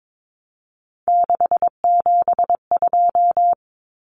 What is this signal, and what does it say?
Morse code, heard as a keyed tone, about 22 wpm, 700 Hz. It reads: 672
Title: Numbers - 3-Digit 22wpm